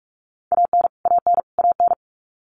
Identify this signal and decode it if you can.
Morse code, heard as a keyed tone, about 44 wpm, 700 Hz. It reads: AN AN AN
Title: Top 500 Words Repeated 45wpm